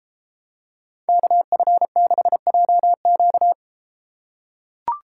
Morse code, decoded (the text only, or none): KF6JQ